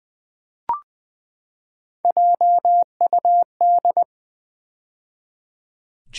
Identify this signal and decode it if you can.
Morse code, heard as a keyed tone, about 20 wpm, 700 Hz. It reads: JUD
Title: Build A Word - 3 Letters At A Time - 8-Letter Words - 20wpm